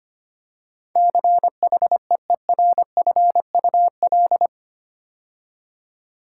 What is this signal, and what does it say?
Morse code, heard as a keyed tone, about 25 wpm, 700 Hz. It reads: CHEERFUL
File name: 8 Letter Words - First 3 Letters Given - 25wpm